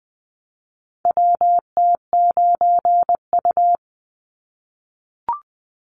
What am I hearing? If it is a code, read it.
Morse code, heard as a keyed tone, about 20 wpm, 700 Hz. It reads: WT9U
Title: Call Signs - 20wpm